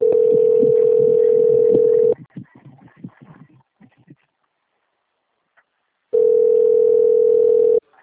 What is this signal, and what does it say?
Ringback tone (North America), 440 + 480 Hz.
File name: out- - - - - .45265